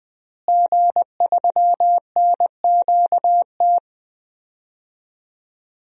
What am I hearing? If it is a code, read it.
Morse code, heard as a keyed tone, about 20 wpm, 700 Hz. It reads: G3NQT
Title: DXCC-100 Call Signs - 5 Characters - Repeated 2x - 20wpm